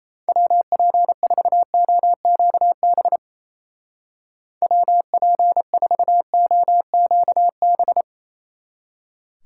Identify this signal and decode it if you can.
Morse code, heard as a keyed tone, about 34 wpm, 700 Hz. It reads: WP4OQB WP4OQB
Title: Speed Racing - USA 2x3 Call Signs 28wpm